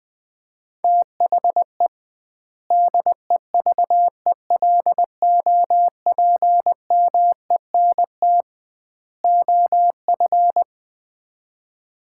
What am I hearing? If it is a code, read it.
Morse code, heard as a keyed tone, about 21 wpm, 700 Hz. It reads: THE DEVELOPMENT OF
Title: 2x Word-Spacing - Sets of 3 Words - 20wpm